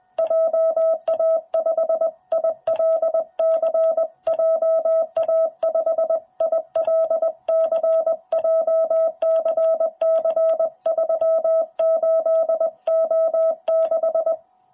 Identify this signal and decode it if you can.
Morse code, heard as a keyed tone, about 20 wpm, 660 Hz. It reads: JA5ILC JA5ILC JCC38O6
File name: jh8cbh / 2021cw